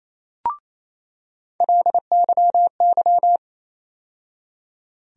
Morse code, heard as a keyed tone, about 27 wpm, 700 Hz. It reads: LYY
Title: USA Call Sign Postfixes - 3 Letters 28wpm